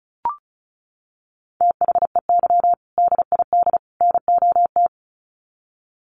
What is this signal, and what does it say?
Morse code, heard as a keyed tone, about 34 wpm, 700 Hz. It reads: THEY DID NOT
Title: Sets of 3 Words 35wpm